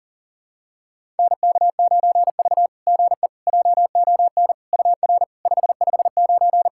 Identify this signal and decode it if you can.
Morse code, heard as a keyed tone, about 44 wpm, 690 Hz. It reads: NK9V GE JON UR 559
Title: SOTA QSO Practice 40wpm